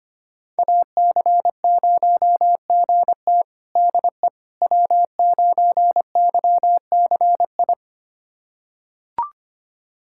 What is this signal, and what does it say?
Morse code, heard as a keyed tone, about 25 wpm, 700 Hz. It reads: AC0GT DE W9YCI
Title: USA Callsign DE Callsign - 5 Characters - 25wpm